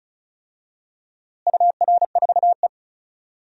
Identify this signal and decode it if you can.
Morse code, heard as a keyed tone, about 36 wpm, 700 Hz. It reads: UR4E
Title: DXCC-100 Call Signs - 4 Characters - Repeated 2x - 35wpm